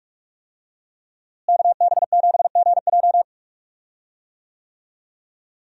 Morse code, heard as a keyed tone, about 46 wpm, 700 Hz. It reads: KB7GJ
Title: USA Call Signs - 5 Characters - Repeated 3x - 45wpm